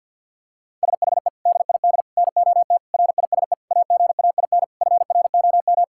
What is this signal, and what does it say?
Morse code, heard as a keyed tone, about 52 wpm, 700 Hz. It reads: SHE DID NOT RISE AGAIN FROM
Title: Sentences from Top 1000 Words - Review Part 6 - 50wpm